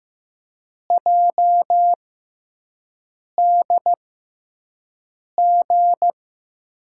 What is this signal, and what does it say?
Morse code, heard as a keyed tone, about 16 wpm, spaced as for 5 wpm, 700 Hz. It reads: JDG
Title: USA Call Signs - 6x Character Spacing - 15wpm